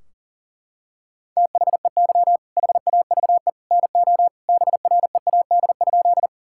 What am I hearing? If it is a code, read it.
Morse code, heard as a keyed tone, about 40 wpm, 700 Hz. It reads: THEY HAVE NO BREAD?
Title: Sentences from Top 1600 Words 40wpm